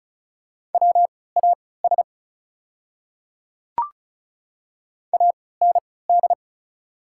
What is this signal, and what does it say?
Morse code, heard as a keyed tone, about 36 wpm, spaced as for 20 wpm, 700 Hz. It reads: WAS AND
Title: 3x Character Spacing - Top 100 Words - 35wpm